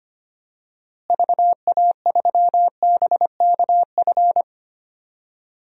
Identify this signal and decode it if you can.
Morse code, heard as a keyed tone, about 25 wpm, 700 Hz. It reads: VA3BKF
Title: DXCC-100 Call Signs - 6 Characters - Repeated 3x - 25wpm